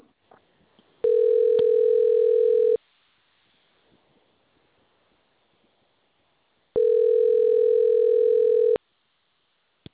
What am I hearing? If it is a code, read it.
Ringback tone (North America), 440 + 480 Hz.